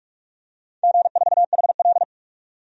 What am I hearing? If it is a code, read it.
Morse code, heard as a keyed tone, about 41 wpm, 700 Hz. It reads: G4HL
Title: DXCC-100 Call Signs - 4 Characters - 45wpm